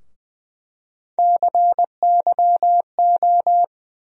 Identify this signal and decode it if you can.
Morse code, heard as a keyed tone, about 20 wpm, 700 Hz. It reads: CYO
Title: USA Call Sign Postfixes - 3 Letters 20wpm